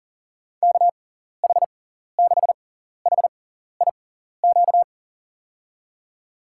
Morse code, heard as a keyed tone, about 40 wpm, spaced as for 14 wpm, 700 Hz. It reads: KH6HIQ
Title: USA Call Signs - 6 Characters - 6x Character Spacing - 40wpm